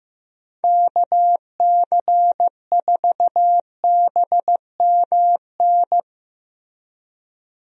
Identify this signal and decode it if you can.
Morse code, heard as a keyed tone, about 15 wpm, 700 Hz. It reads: KC4BMN
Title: USA Call Signs - 6 Characters - Repeated 3x - 15wpm